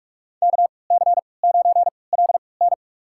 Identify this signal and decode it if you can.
Morse code, heard as a keyed tone, about 44 wpm, spaced as for 25 wpm, 700 Hz. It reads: KC9LN
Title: USA Call Signs - 5 Characters - 3x Character Spacing - 45wpm